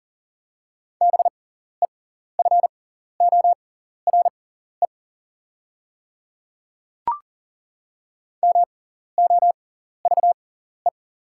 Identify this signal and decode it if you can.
Morse code, heard as a keyed tone, about 40 wpm, spaced as for 14 wpm, 700 Hz. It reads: BEFORE MOVE
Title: 6x Character Spacing - Top 200 Words - 40wpm